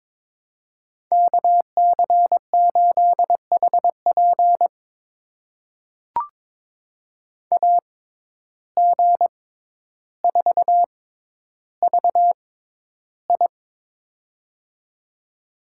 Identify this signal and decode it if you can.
Morse code, heard as a keyed tone, about 22 wpm, spaced as for 11 wpm, 700 Hz. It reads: KC8HP AG4VI
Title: USA Call Signs - 5 Characters - 6x Character Spacing - 22wpm